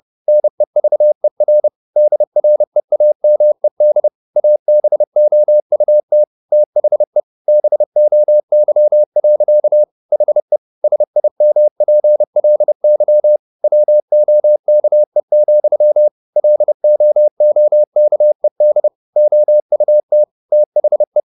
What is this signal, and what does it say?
Morse code, heard as a keyed tone, about 30 wpm, 600 Hz. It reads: NEVER DREAMED ABOUT THE BOY. HE SIMPLY WOKE, LOOKED OUT THE
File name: the old man and the sea / the old man and the sea 1 30wpm